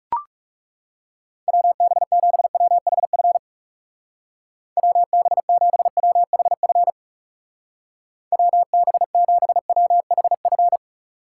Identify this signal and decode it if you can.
Morse code, heard as a keyed tone, about 36 wpm, 700 Hz. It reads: WB7WHF WB7WHF WB7WHF
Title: Speed Racing - USA 2x3 Call Signs 30wpm